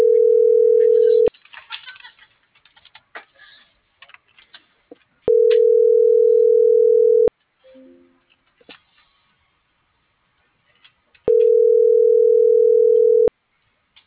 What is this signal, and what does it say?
Ringback tone (North America), 440 + 480 Hz.